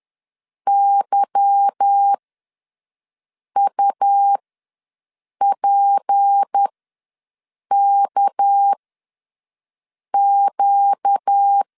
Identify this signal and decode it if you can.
Morse code, heard as a keyed tone, about 10 wpm, spaced as for 5 wpm, 800 Hz. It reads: YUPKQ